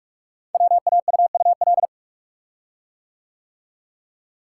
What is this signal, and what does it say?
Morse code, heard as a keyed tone, about 44 wpm, 700 Hz. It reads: WAUUL
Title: Code Groups - 5 Letters - 45wpm